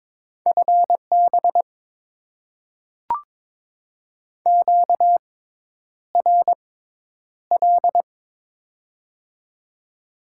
Morse code, heard as a keyed tone, about 22 wpm, spaced as for 10 wpm, 690 Hz. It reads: FB QRL
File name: Top 20 QSO Elements - 6x Character Spacing - 22wpm